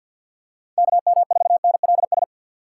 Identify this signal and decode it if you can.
Morse code, heard as a keyed tone, about 49 wpm, 700 Hz. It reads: KM4NLS
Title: USA Call Signs - 6 Characters - 6x Character Spacing - 50wpm